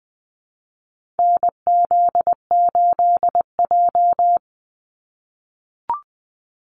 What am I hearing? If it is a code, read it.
Morse code, heard as a keyed tone, about 20 wpm, 700 Hz. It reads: NZ8J
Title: Call Signs - 20wpm